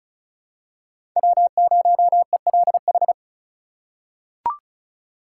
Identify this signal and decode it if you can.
Morse code, heard as a keyed tone, about 36 wpm, 700 Hz. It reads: W0ELH
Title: USA Call Signs - 2x Character Spacing - 35wpm